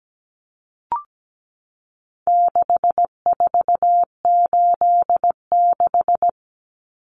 Morse code, heard as a keyed tone, about 17 wpm, 700 Hz. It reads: 6486
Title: Numbers - 4-Digit 17wpm